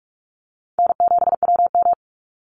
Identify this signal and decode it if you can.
Morse code, heard as a keyed tone, about 44 wpm, 700 Hz. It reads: N7WM
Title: NA SOTA Chaser Callsigns 45wpm